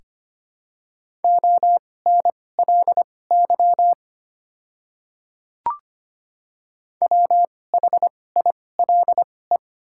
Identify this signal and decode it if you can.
Morse code, heard as a keyed tone, about 25 wpm, spaced as for 18 wpm, 700 Hz. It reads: ONLY WHILE
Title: 2x Character Spacing - Top 300 Words - 25wpm